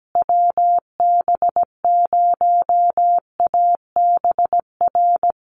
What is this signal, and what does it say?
Morse code, heard as a keyed tone, about 17 wpm, 700 Hz. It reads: WB0ABR
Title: USA Call Signs Repeated 3x 17wpm